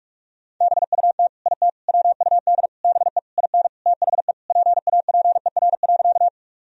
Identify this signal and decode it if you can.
Morse code, heard as a keyed tone, about 43 wpm, 700 Hz. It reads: BUT IT WUD BE IN THE PAPER.
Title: Sentences from Top 300 Words and Abbreviations 45wpm